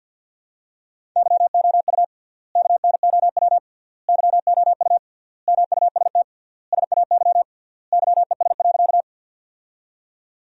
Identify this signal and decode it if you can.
Morse code, heard as a keyed tone, about 50 wpm, 700 Hz. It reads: YOU KNOW YOU MUST SAY YES.
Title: 3x Word-Spacing - Sentences from Top 500 Words - 50wpm